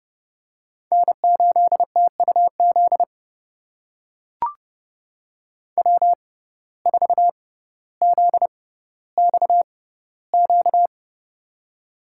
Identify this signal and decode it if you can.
Morse code, heard as a keyed tone, about 30 wpm, spaced as for 15 wpm, 690 Hz. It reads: N8TUZ W4ZXQ
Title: USA Call Signs - 5 Characters - 6x Character Spacing - 30wpm